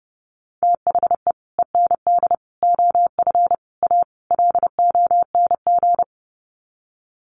Morse code, heard as a keyed tone, about 30 wpm, 700 Hz. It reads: THE END OF A LONG
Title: Sets of 5 Words Encore2 30wpm